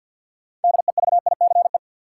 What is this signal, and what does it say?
Morse code, heard as a keyed tone, about 47 wpm, 700 Hz. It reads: DEVICE
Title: Sentences from Top 1500 Words - New Word Drill - 50wpm